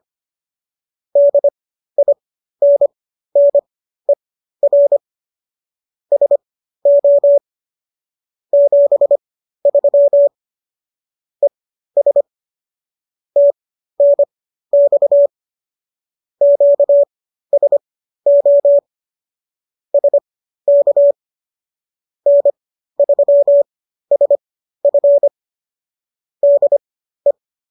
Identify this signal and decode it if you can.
Morse code, heard as a keyed tone, about 25 wpm, spaced as for 13 wpm, 570 Hz. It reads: DINNER SO 73 ES TNX QSO SK N3SF DE